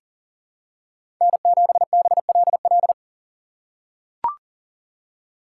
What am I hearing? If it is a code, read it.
Morse code, heard as a keyed tone, about 40 wpm, 700 Hz. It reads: N7BLL
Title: USA Call Signs - 5 Characters - 3x Character Spacing - 40wpm